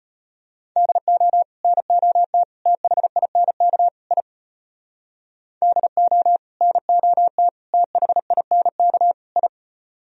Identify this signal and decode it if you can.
Morse code, heard as a keyed tone, about 33 wpm, 700 Hz. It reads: DO NOT THINK I DO NOT THINK I